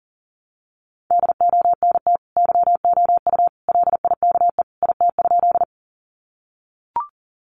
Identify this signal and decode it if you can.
Morse code, heard as a keyed tone, about 40 wpm, 700 Hz. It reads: DONT YOU LIKE IT?
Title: Sentences from Top 300 Words 40wpm